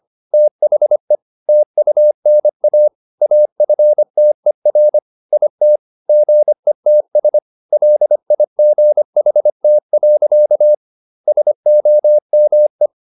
Morse code, heard as a keyed tone, about 25 wpm, 600 Hz. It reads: THE TUNA AFTER IT GETS LIGHT. SOME
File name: the old man and the sea part 1 25wpm